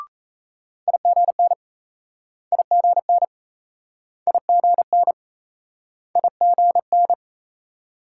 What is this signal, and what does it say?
Morse code, heard as a keyed tone, about 38 wpm, 700 Hz. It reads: IGN IGN IGN IGN